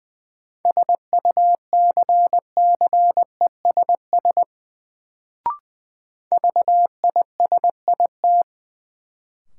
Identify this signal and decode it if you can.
Morse code, heard as a keyed tone, about 20 wpm, 700 Hz. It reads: SUCCESS VISIT
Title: Sentences from Top 900 Words - New Word Drill - 20wpm